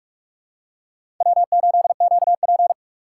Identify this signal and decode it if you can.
Morse code, heard as a keyed tone, about 43 wpm, 700 Hz. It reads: W8QP
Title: USA Call Signs - 4x Character Spacing - 45wpm